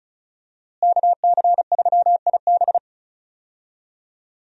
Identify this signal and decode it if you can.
Morse code, heard as a keyed tone, about 36 wpm, 700 Hz. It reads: KC3IB
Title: USA Call Signs - 5 Characters - Repeated 3x - 35wpm